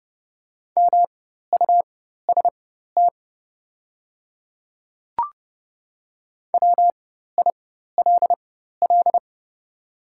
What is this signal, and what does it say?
Morse code, heard as a keyed tone, about 30 wpm, spaced as for 14 wpm, 700 Hz. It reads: MUST WILL
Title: 4x Character Spacing - Top 200 Words - 30wpm